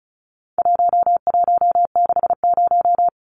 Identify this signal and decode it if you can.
Morse code, heard as a keyed tone, about 34 wpm, 700 Hz. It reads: 1160
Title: Numbers - 4-Digit 35wpm